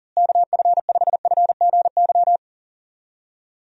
Morse code, heard as a keyed tone, about 40 wpm, 700 Hz. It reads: KF5FGY
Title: USA Call Signs - 6 Characters - Repeated 2x - 40wpm